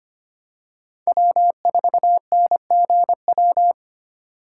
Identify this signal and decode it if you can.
Morse code, heard as a keyed tone, about 25 wpm, 700 Hz. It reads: W4NGW
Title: USA Call Signs - 5 Characters - 3x Character Spacing - 25wpm